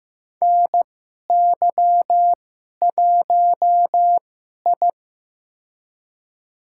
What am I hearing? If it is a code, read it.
Morse code, heard as a keyed tone, about 15 wpm, spaced as for 11 wpm, 700 Hz. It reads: NY1I
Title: USA Call Signs - 2x Character Spacing - 15wpm